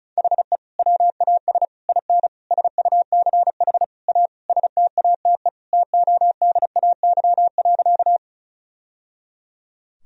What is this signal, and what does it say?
Morse code, heard as a keyed tone, about 34 wpm, 700 Hz. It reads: HE WAS IN SUCH A STATE TODAY.
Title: Sentences from Top 1000 Words - Review Part 2 - 35wpm